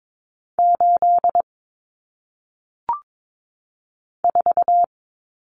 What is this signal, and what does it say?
Morse code, heard as a keyed tone, about 22 wpm, 700 Hz. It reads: 8 4
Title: Numbers - 1-Digit 22wpm